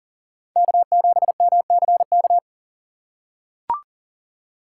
Morse code, heard as a keyed tone, about 40 wpm, 700 Hz. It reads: K7MCK
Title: USA Call Signs - 3x Character Spacing - 40wpm